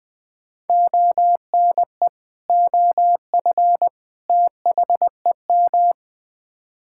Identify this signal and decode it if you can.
Morse code, heard as a keyed tone, about 20 wpm, 700 Hz. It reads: ONE OF THEM
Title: Build A Phrase - 3-Word Phrases - 20wpm